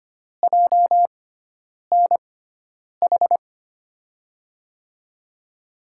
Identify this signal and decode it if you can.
Morse code, heard as a keyed tone, about 25 wpm, spaced as for 9 wpm, 700 Hz. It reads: JNH
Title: USA Call Signs - 6 Characters - 6x Character Spacing - 25wpm